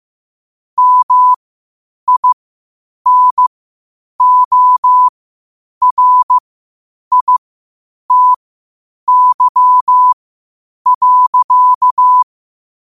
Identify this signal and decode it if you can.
Morse code, heard as a keyed tone, about 15 wpm, spaced as for 8 wpm, 1000 Hz. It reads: MINORITY.
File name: qotdfiles / 8wpm